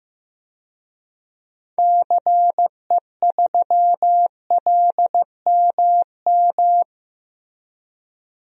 Morse code, heard as a keyed tone, about 14 wpm, 700 Hz. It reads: CE3LMM
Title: DXCC-100 Call Signs - 6 Characters - Repeated 3x - 15wpm